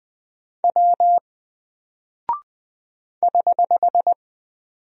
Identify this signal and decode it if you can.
Morse code, heard as a keyed tone, about 20 wpm, 700 Hz. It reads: W <HH>
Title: Single Character (Letter:Number:Punctuation:Prosign) Alt 20wpm